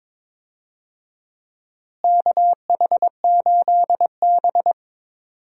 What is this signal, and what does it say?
Morse code, heard as a keyed tone, about 22 wpm, 700 Hz. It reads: KH8B
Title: USA Call Signs - 4 Characters - Repeated 2x - 22wpm